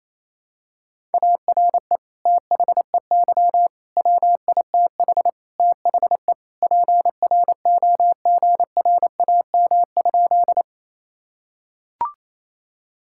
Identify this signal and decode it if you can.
Morse code, heard as a keyed tone, about 28 wpm, 700 Hz. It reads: ARE THEY WITH THE PROGRAM?
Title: Sentences from Top 1000 Words - Review Part 5 - 28wpm